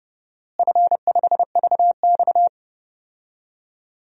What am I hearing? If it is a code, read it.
Morse code, heard as a keyed tone, about 30 wpm, 700 Hz. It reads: F5VX
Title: DXCC-100 Call Signs - 4 Characters - Repeated 3x - 30wpm